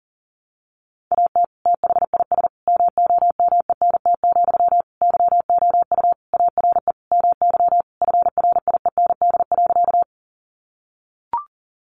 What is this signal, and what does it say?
Morse code, heard as a keyed tone, about 40 wpm, 700 Hz. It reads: AT THIS MOMENT, YOU ARE MY FRIEND.
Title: Sentences from Top 1000 Words 40wpm